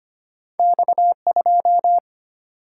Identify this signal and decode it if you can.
Morse code, heard as a keyed tone, about 25 wpm, 700 Hz. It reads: X2
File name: Two Letter-Number Combinations - 25wpm